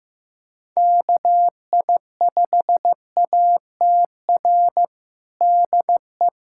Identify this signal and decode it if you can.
Morse code, heard as a keyed tone, about 15 wpm, 700 Hz. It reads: KI5ATR DE
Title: USA Callsign DE Callsign 15wpm